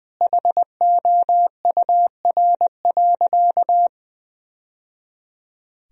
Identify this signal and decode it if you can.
Morse code, heard as a keyed tone, about 20 wpm, 700 Hz. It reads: HOUR.
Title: Sentences from Top 1000 Words - Review Part 1 - 20wpm